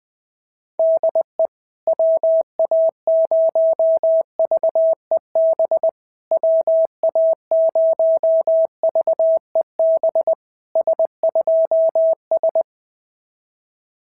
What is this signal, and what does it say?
Morse code, heard as a keyed tone, about 20 wpm, 650 Hz. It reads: DE WA0VEB WA0VEB S2S
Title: SOTA QSO Practice 20wpm